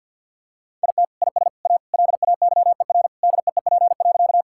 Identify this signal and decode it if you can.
Morse code, heard as a keyed tone, about 50 wpm, 700 Hz. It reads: IT IS A LAYER DEEP.
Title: Sentences from Top 1600 Words 50wpm